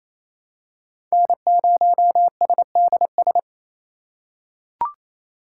Morse code, heard as a keyed tone, about 28 wpm, 700 Hz. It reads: N0SDS
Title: USA Call Signs - 5 Characters - 2x Character Spacing - 28wpm